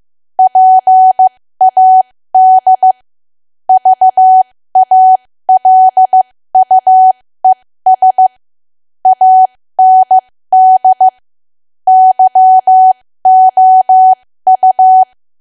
Morse code, heard as a keyed tone, about 15 wpm, spaced as for 13 wpm, 750 Hz. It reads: PAD VALUES AND YOU
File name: cwa / W1AW-Level2 / 13WPM